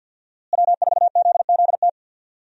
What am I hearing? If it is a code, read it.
Morse code, heard as a keyed tone, about 46 wpm, 700 Hz. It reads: W4ZZT
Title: USA Call Signs - 5 Characters - Repeated 3x - 50wpm